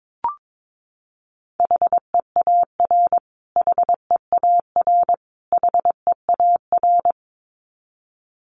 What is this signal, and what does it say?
Morse code, heard as a keyed tone, about 22 wpm, 700 Hz. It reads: HEAR HEAR HEAR
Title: Top 500 Words Repeated 22wpm